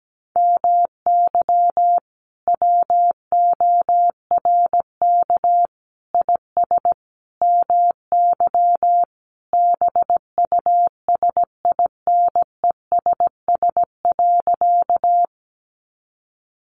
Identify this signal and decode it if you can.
Morse code, heard as a keyed tone, about 17 wpm, 700 Hz. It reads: MY WORK IS MY BUSINESS.
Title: Sentences from Top 600 Words 17wpm